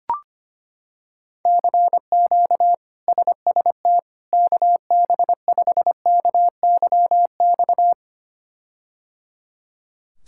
Morse code, heard as a keyed tone, about 25 wpm, 700 Hz. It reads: CQ SST KB5KYX
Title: K1USN SST QSO Practice - 25wpm